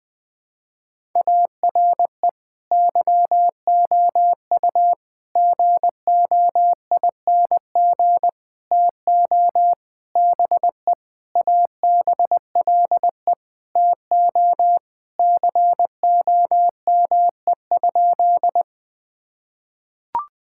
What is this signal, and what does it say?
Morse code, heard as a keyed tone, about 20 wpm, 700 Hz. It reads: ARE YOU GOING TO BE ABLE TO COME?
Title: Sentences with 5-Word Phrases - 20wpm